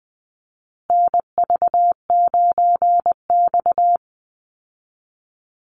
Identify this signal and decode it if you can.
Morse code, heard as a keyed tone, about 20 wpm, 700 Hz. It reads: NV9X
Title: Call Signs Repeated 3x - 20wpm